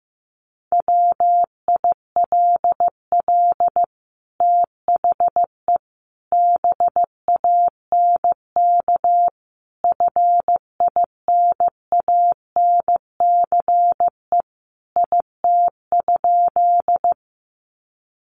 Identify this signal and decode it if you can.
Morse code, heard as a keyed tone, about 15 wpm, 700 Hz. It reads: WILL THE BANK FINANCE IT?
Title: Sentences from Top 1500 Words 15wpm